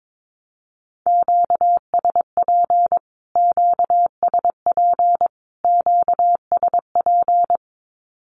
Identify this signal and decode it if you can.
Morse code, heard as a keyed tone, about 22 wpm, 700 Hz. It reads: QSP QSP QSP
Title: Q-Codes Repeated 3x 22wpm